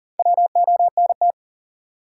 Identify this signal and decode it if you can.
Morse code, heard as a keyed tone, about 40 wpm, 700 Hz. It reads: WONT
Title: Sentences from Top 1000 Words - Review Part 6 - New Word Drill - 40wpm